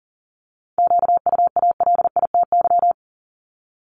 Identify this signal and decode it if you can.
Morse code, heard as a keyed tone, about 39 wpm, 690 Hz. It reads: QUALITY
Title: Multi-letter Phonemes 40wpm